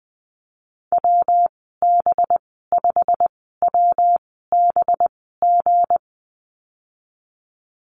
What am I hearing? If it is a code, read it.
Morse code, heard as a keyed tone, about 20 wpm, spaced as for 15 wpm, 700 Hz. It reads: WB5WBG
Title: 193 Lesson 31g - Letter v - Call Signs 20wpm